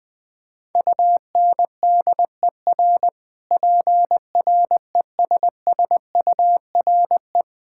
Morse code, heard as a keyed tone, about 21 wpm, 700 Hz. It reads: UNDER PRESSURE
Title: Sentences from Top 1000 Words - Review Part 6 - 20wpm